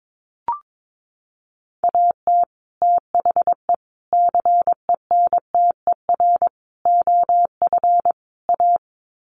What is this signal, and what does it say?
Morse code, heard as a keyed tone, about 22 wpm, 700 Hz. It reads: AT THE CENTER OF A
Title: Sets of 5 Words Encore2 22wpm